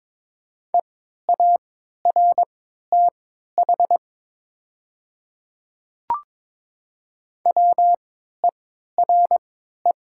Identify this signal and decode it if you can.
Morse code, heard as a keyed tone, about 22 wpm, spaced as for 12 wpm, 700 Hz. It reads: EARTH WERE
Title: 3x Character Spacing - Top 200 Words - 22wpm